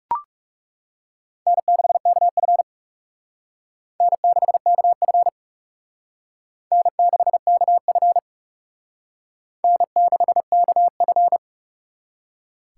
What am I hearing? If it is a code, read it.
Morse code, heard as a keyed tone, about 37 wpm, 700 Hz. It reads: N6KF N6KF N6KF N6KF
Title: Speed Racing - USA 1x2 Call Signs 30wpm